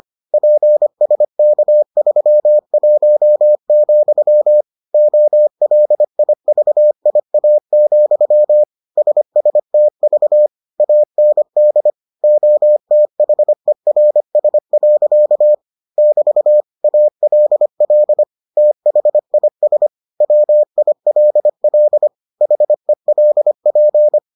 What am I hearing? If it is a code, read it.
Morse code, heard as a keyed tone, about 25 wpm, 600 Hz. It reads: PSK31, OLIVIA, SSTV AND OTHERS. <BT> ALL THIS WILL HELP